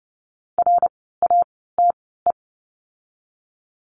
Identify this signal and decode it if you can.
Morse code, heard as a keyed tone, about 30 wpm, spaced as for 17 wpm, 700 Hz. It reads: RATE
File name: Lesson 17f - Letter u - Words 30wpm